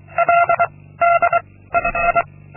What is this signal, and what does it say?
Morse code, heard as a keyed tone, about 24 wpm, 690 Hz. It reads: L D II
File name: Linksmakalnis-2